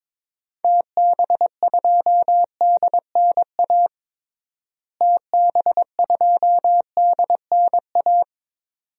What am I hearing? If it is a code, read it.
Morse code, heard as a keyed tone, about 21 wpm, 700 Hz. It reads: TB2DNA TB2DNA
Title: DXCC-100 Call Signs - 6 Characters - Repeated 2x - 22wpm